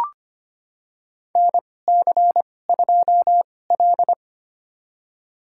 Morse code, heard as a keyed tone, about 25 wpm, spaced as for 18 wpm, 700 Hz. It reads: NC2L
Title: USA Call Signs - 2x Character Spacing - 25wpm